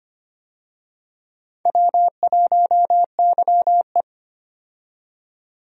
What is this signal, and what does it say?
Morse code, heard as a keyed tone, about 25 wpm, 700 Hz. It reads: W1YE
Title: USA Call Signs - 4 Characters - 25wpm